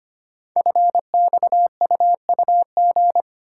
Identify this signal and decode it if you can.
Morse code, heard as a keyed tone, about 25 wpm, 700 Hz. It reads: FXUUG
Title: Code Groups - 5 Letters - 25wpm